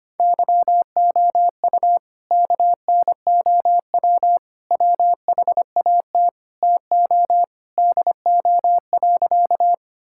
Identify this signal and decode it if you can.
Morse code, heard as a keyed tone, about 25 wpm, 700 Hz. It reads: YOU KNOW WHAT TO DO.
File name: practice / 25wpm / Sentences with 3-Word Phrases - 25wpm